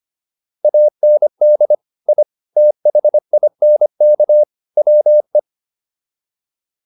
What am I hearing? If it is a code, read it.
Morse code, heard as a keyed tone, about 25 wpm, 600 Hz. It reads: AND I THINK WE